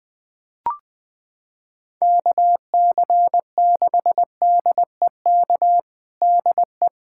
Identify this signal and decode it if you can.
Morse code, heard as a keyed tone, about 21 wpm, 700 Hz. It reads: KC6DEK DE
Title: USA Callsign DE Callsign - 6 Characters - 20wpm